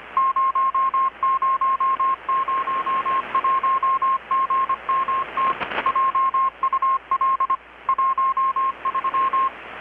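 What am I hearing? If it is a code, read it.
Morse code, heard as a keyed tone, about 25 wpm, 1070 Hz. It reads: 0001GMT JUL 13
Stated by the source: Morse from KPH's Night of Nights XX maritime radiotelegraphy event, received in Chiapas, Mexico about 3,800 km away on a Kenwood R-600 with a grounded 10 m long-wire antenna.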